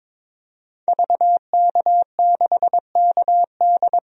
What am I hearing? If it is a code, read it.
Morse code, heard as a keyed tone, about 22 wpm, 700 Hz. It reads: VK6KD
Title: DXCC-100 Call Signs - 5 Characters - 22wpm